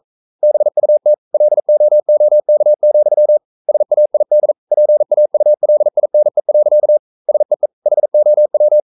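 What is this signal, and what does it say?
Morse code, heard as a keyed tone, about 42 wpm, 600 Hz. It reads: BUT LOOK, SAID PAULINE. SEE HOW
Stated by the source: Morse code practice at 41 wpm, from CW med Gnister.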